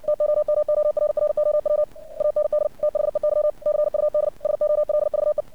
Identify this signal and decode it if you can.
Morse code, heard as a keyed tone, about 56 wpm, 600 Hz. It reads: TOMORROW AND THEY CUD SOLVE
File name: archived / CRSnet